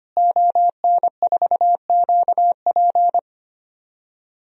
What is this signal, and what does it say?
Morse code, heard as a keyed tone, about 25 wpm, 700 Hz. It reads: ON4QP
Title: DXCC-100 Call Signs - 5 Characters - Repeated 2x - 25wpm